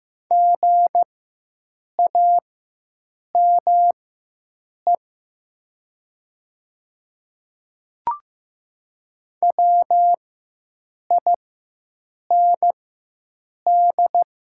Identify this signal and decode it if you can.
Morse code, heard as a keyed tone, about 15 wpm, spaced as for 7 wpm, 700 Hz. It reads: GAME WIND
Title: 4x Character Spacing - Top 500 Words - 15wpm